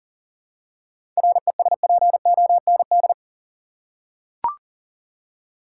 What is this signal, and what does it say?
Morse code, heard as a keyed tone, about 40 wpm, 700 Hz. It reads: RESPOND
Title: Seven Letter Words - 40wpm